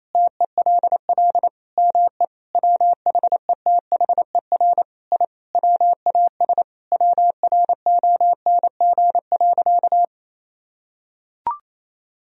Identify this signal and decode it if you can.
Morse code, heard as a keyed tone, about 27 wpm, 700 Hz. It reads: TELL ME WHETHER I WAS WRONG.
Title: Sentences from Top 1000 Words - Review Part 3 - 28wpm